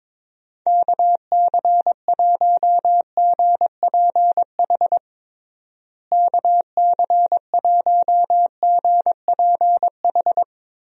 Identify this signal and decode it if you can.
Morse code, heard as a keyed tone, about 22 wpm, 700 Hz. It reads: KC1GPH KC1GPH
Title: USA Call Signs - 6 Characters - Repeated 3x - 22wpm